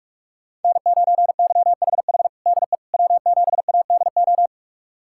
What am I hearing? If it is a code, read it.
Morse code, heard as a keyed tone, about 42 wpm, 700 Hz. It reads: N9YHH DE W7ADO
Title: USA Callsign DE Callsign - 5 Characters - 45wpm